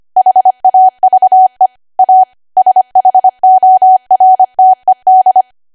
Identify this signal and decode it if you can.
Morse code, heard as a keyed tone, about 25 wpm, 750 Hz. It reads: HAVE A SHORTED
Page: 25 WPM Code Archive